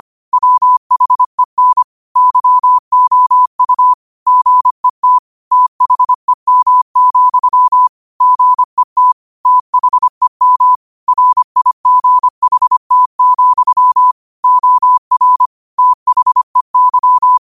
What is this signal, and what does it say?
Morse code, heard as a keyed tone, about 25 wpm, 1000 Hz. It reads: WHEN YOU GET THEM, GET THEM RIGHT, OR THEY